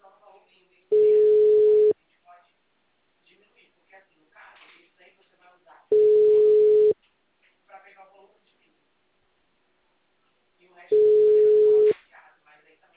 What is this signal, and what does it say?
Ringback tone (Europe, 425 Hz), with speech.